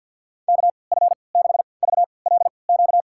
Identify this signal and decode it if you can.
Morse code, heard as a keyed tone, about 52 wpm, spaced as for 29 wpm, 700 Hz. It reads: KF6VLQ
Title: USA Call Signs - 6 Characters - 3x Character Spacing - 50wpm